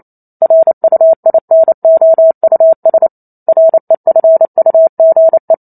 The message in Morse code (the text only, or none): RUINOUS REFUGE